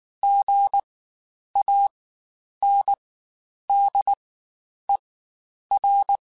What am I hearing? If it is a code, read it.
Morse code, heard as a keyed tone, about 19 wpm, spaced as for 9 wpm, 800 Hz. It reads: GANDER
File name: Spaced Alphas / Spaced Alphas mp3s / Gs sp @18